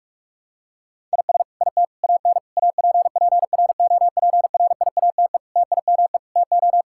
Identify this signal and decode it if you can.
Morse code, heard as a keyed tone, about 46 wpm, 700 Hz. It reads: IS IT AN APPROPRIATE TIME TO
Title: Sentences from Top 1000 Words - Review Part 5 - 45wpm